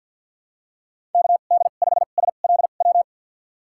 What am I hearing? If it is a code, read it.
Morse code, heard as a keyed tone, about 49 wpm, spaced as for 36 wpm, 690 Hz. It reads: KD5SLW
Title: USA Call Signs - 2x Character Spacing - 50wpm